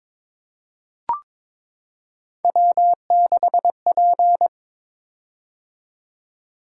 Morse code, heard as a keyed tone, about 23 wpm, 700 Hz. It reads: W6P
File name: practice / 22wpm / Three Letter-Number Combinations - 22wpm